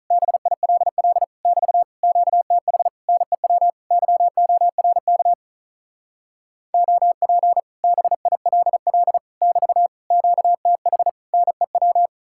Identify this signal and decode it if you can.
Morse code, heard as a keyed tone, about 45 wpm, 700 Hz. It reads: BILL <BT> QTH NEW YORK OP BILL <BT> QTH NEW